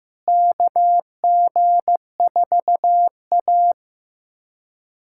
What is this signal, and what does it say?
Morse code, heard as a keyed tone, about 15 wpm, 700 Hz. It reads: KG4A
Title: USA Callsign DE Callsign - 4 Characters - 15wpm